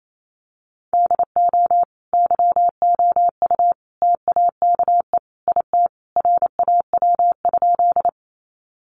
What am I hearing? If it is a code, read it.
Morse code, heard as a keyed tone, about 28 wpm, 700 Hz. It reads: DO YOU TAKE IT RAW?
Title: Sentences from Top 1500 Words 28wpm